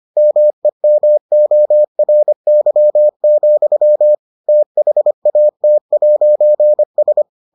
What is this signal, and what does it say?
Morse code, heard as a keyed tone, about 25 wpm, 600 Hz. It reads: MEMORY, THAT'S